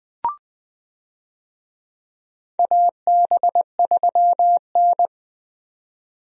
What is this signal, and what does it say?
Morse code, heard as a keyed tone, about 20 wpm, 700 Hz. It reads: AB3N
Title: USA Call Signs - 4 Characters - 20wpm